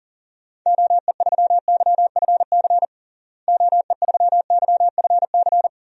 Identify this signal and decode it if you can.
Morse code, heard as a keyed tone, about 40 wpm, 700 Hz. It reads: OE3YFC OE3YFC
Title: DXCC-100 Call Signs - 6 Characters - Repeated 2x - 40wpm